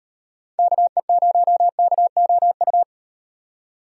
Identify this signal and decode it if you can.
Morse code, heard as a keyed tone, about 38 wpm, 700 Hz. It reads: KE0KOU